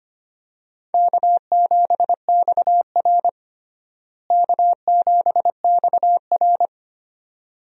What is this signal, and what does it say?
Morse code, heard as a keyed tone, about 25 wpm, 700 Hz. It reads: K7XR K7XR
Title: USA Call Signs - 4 Characters - Repeated 2x - 25wpm